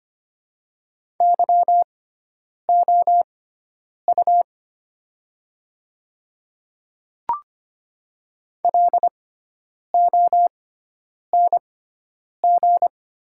Morse code, heard as a keyed tone, about 25 wpm, spaced as for 9 wpm, 690 Hz. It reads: YOU LONG
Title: 6x Character Spacing - Top 200 Words - 25wpm